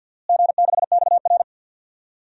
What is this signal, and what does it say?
Morse code, heard as a keyed tone, about 46 wpm, 700 Hz. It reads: G6XR
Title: DXCC-100 Call Signs - 4 Characters - Repeated 2x - 50wpm